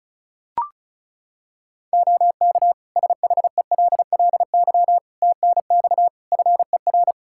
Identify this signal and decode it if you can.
Morse code, heard as a keyed tone, about 34 wpm, 700 Hz. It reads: OK SHELLY TNX FER
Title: Practice QSO - Exchange 4 - Ending - 35wpm